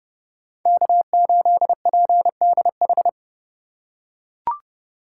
Morse code, heard as a keyed tone, about 32 wpm, 700 Hz. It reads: K8PDH
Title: USA Call Signs - 5 Characters - 4x Character Spacing - 30wpm